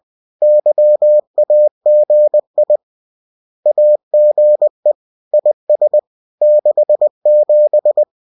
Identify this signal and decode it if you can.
Morse code, heard as a keyed tone, about 20 wpm, 600 Hz. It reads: YAGI AGE IS 67